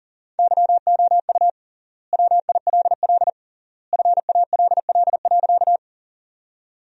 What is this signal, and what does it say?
Morse code, heard as a keyed tone, about 37 wpm, 700 Hz. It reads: YOU WILL FALL.
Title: 3x Word-Spacing - Sentences from Top 500 Words - 40wpm